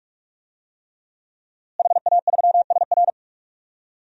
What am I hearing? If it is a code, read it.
Morse code, heard as a keyed tone, about 44 wpm, 700 Hz. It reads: HA3SR
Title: DXCC-100 Call Signs - 5 Characters - Repeated 3x - 45wpm